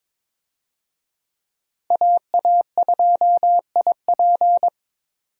Morse code, heard as a keyed tone, about 22 wpm, 700 Hz. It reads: AA2IP
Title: USA Call Signs - 5 Characters - 22wpm